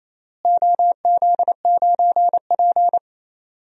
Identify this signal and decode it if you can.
Morse code, heard as a keyed tone, about 27 wpm, 700 Hz. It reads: OZ9P
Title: DXCC-100 Call Signs - 4 Characters - Repeated 3x - 28wpm